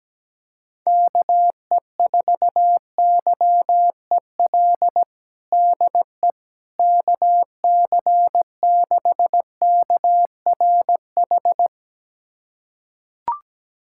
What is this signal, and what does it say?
Morse code, heard as a keyed tone, about 17 wpm, 700 Hz. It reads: KE4YEL DE KC6KRH
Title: USA Callsign DE Callsign 17wpm